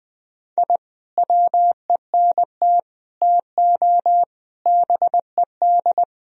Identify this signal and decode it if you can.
Morse code, heard as a keyed tone, about 20 wpm, 700 Hz. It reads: I WENT TO BED
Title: Sentences from Top 1000 Words - Review Part 2 - 20wpm